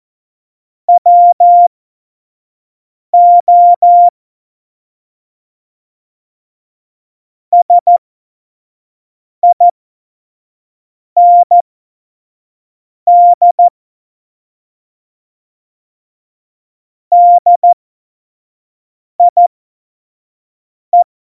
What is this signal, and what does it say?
Morse code, heard as a keyed tone, about 14 wpm, spaced as for 5 wpm, 700 Hz. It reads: WO SIND DIE
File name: Morsekurs / Witze / Witz 19. September 17, Tempo 25